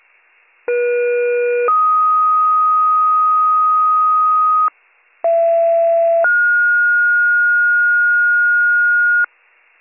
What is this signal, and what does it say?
Two-tone page, heard as a one-second tone, then a three-second tone, about 480 then 1250 Hz, about 670 then 1430 Hz, with noise. It is heard from 0.7 to 9.2 s.